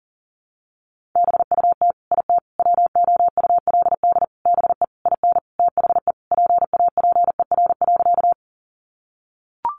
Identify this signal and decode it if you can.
Morse code, heard as a keyed tone, about 37 wpm, 700 Hz. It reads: BUT IT WOULD BE IN THE PAPER.
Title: Sentences from Top 300 Words 40wpm